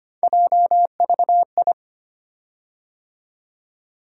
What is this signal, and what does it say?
Morse code, heard as a keyed tone, about 25 wpm, 700 Hz. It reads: JVI
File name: Three Letter-Number Combinations - 25wpm